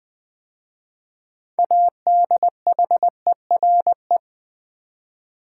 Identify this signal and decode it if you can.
Morse code, heard as a keyed tone, about 20 wpm, 700 Hz. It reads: ADHERE
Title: 6 Letter Words - First 3 Letters Given - 20wpm